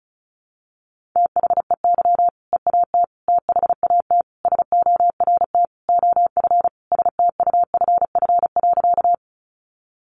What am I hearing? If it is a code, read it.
Morse code, heard as a keyed tone, about 34 wpm, 700 Hz. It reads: THEY EAT THAT SORT OF STUFF.
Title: Sentences from Top 1400 Words 35wpm